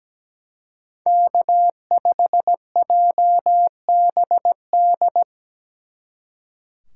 Morse code, heard as a keyed tone, about 19 wpm, 700 Hz. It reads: K5JBD
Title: Speed Racing - USA Call Signs 17wpm